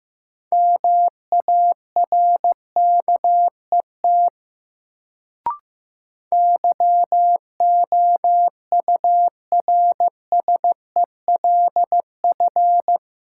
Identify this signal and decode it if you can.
Morse code, heard as a keyed tone, about 15 wpm, 700 Hz. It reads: MARKET YOURSELF
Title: Sentences from Top 600 Words - New Word Drill - 15wpm